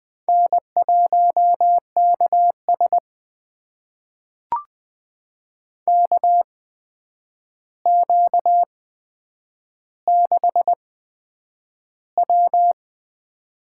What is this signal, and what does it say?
Morse code, heard as a keyed tone, about 20 wpm, spaced as for 9 wpm, 700 Hz. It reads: N1KS KQ6W
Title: USA Call Signs - 4 Characters - 8x Character Spacing - 20wpm